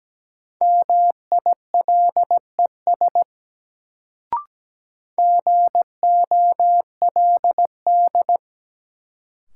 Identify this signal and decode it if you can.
Morse code, heard as a keyed tone, about 17 wpm, 700 Hz. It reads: MILES GOLD
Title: Sentences from Top 1000 Words - Review Part 3 - New Word Drill - 17wpm